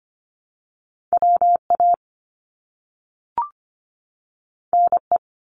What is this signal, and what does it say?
Morse code, heard as a keyed tone, about 25 wpm, 700 Hz. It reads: WA NE
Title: US States 25wpm